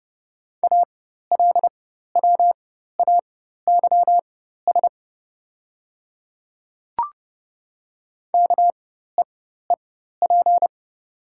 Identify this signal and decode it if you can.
Morse code, heard as a keyed tone, about 29 wpm, spaced as for 14 wpm, 700 Hz. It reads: ALWAYS KEEP
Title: 4x Character Spacing - Top 300 Words - 30wpm